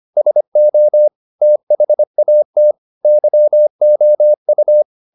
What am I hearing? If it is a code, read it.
Morse code, heard as a keyed tone, about 25 wpm, 600 Hz. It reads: SO THAT YOU
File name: the old man and the sea part 1 25wpm